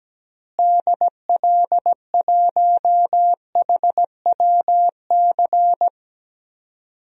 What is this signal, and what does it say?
Morse code, heard as a keyed tone, about 17 wpm, 700 Hz. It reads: DL1HWC
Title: DXCC-100 Call Signs - 6 Characters - Repeated 3x - 17wpm